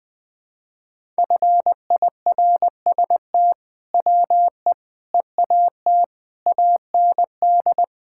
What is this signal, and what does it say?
Morse code, heard as a keyed tone, about 20 wpm, 700 Hz. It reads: FIRST WE EAT AND
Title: Sentences with 2-Word Phrases - 20wpm